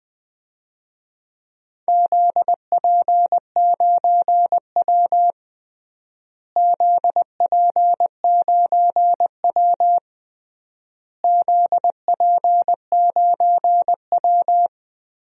Morse code, heard as a keyed tone, about 20 wpm, 700 Hz. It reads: ZP9W ZP9W ZP9W
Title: DXCC-100 Call Signs - 4 Characters - Repeated 3x - 20wpm